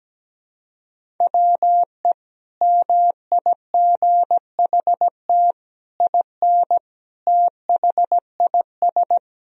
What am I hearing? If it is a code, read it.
Morse code, heard as a keyed tone, about 17 wpm, 700 Hz. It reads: WE MIGHT IN THIS
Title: Sentences from Top 1000 Words - Review Part 1 - 17wpm